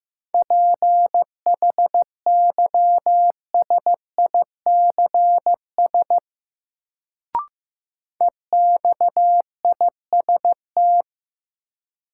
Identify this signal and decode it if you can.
Morse code, heard as a keyed tone, about 15 wpm, 700 Hz. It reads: PHYSICS EXIST
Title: Sentences from Top 1000 Words - Review Part 5 - New Word Drill - 15wpm